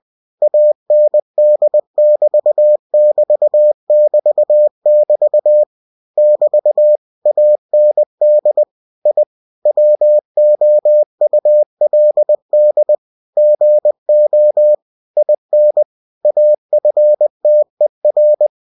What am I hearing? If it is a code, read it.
Morse code, heard as a keyed tone, about 20 wpm, 600 Hz. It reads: AND <BT> <BT> <BT> <BT> <BT> AND I WOULD GO IN AFTER
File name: cw-books / winnie-the-pooh / winnie-20wpm-page-0036